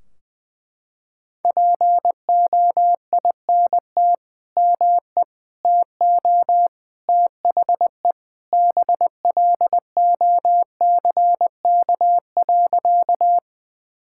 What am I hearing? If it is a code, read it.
Morse code, heard as a keyed tone, about 21 wpm, 700 Hz. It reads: POINT ME TO THE BLOCK.
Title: Sentences from Top 1600 Words 20wpm